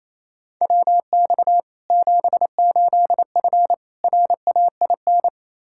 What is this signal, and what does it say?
Morse code, heard as a keyed tone, about 28 wpm, 700 Hz. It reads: WX 78F RAIN
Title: K1USN SST QSO Practice 28wpm